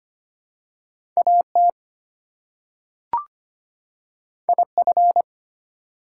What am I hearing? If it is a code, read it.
Morse code, heard as a keyed tone, about 25 wpm, 700 Hz. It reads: AT IF
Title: Two Letter Words - 25wpm